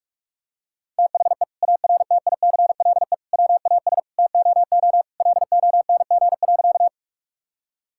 Morse code, heard as a keyed tone, about 46 wpm, 700 Hz. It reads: THE ARTICLE WAS TOO LONG.
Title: Sentences from Top 1000 Words - Review Part 5 - 45wpm